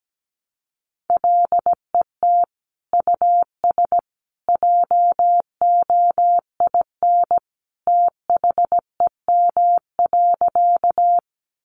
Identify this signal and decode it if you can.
Morse code, heard as a keyed tone, about 17 wpm, 700 Hz. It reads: LET US JOIN THEM.
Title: Sentences from Top 1400 Words 17wpm